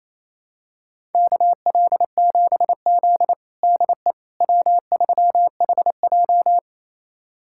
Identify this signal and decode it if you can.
Morse code, heard as a keyed tone, about 28 wpm, 700 Hz. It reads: KL7Z DE W3HJ
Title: USA Callsign DE Callsign - 4 Characters - 28wpm